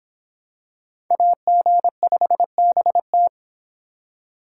Morse code, heard as a keyed tone, about 24 wpm, 700 Hz. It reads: AG5BT